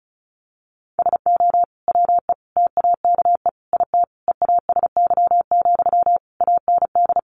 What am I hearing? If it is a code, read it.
Morse code, heard as a keyed tone, about 34 wpm, 700 Hz. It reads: SO WE TAKE IT EASY, AND
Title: Sentences from Top 700 Words 35wpm